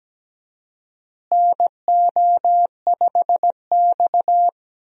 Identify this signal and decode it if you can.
Morse code, heard as a keyed tone, about 17 wpm, 700 Hz. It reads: NO5X
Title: USA Call Signs - 4 Characters - Repeated 2x - 17wpm